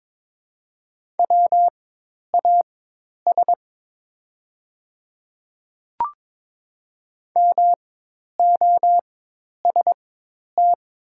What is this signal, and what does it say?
Morse code, heard as a keyed tone, about 22 wpm, spaced as for 10 wpm, 700 Hz. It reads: WAS MOST
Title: 4x Character Spacing - Top 500 Words - 22wpm